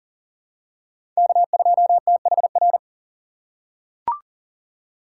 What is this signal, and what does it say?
Morse code, heard as a keyed tone, about 40 wpm, 700 Hz. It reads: K2THR
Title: USA Call Signs - 5 Characters - 2x Character Spacing - 40wpm